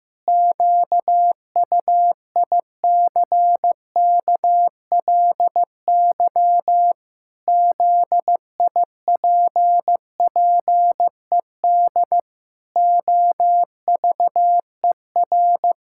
Morse code, heard as a keyed tone, about 15 wpm, 700 Hz. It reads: QUICKLY ZIPPED OVER
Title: Pangram Challenge v2 - 15wpm